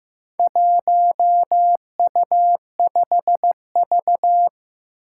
Morse code, heard as a keyed tone, about 15 wpm, 700 Hz. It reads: 1U5V
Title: Four Challenging Letter-Number Combinations 15wpm